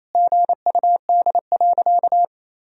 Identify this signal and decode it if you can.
Morse code, heard as a keyed tone, about 27 wpm, 700 Hz. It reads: GUD.
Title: Sentences from Top 200 Words and Abbreviations 28wpm